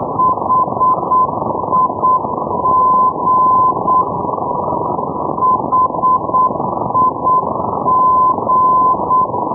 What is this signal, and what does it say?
Morse code, heard as a keyed tone, about 8 wpm, 1020 Hz, with noise. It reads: HIG HIG
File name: europe / d-3500 ndb bremen hig-318-hh-2012-03-rb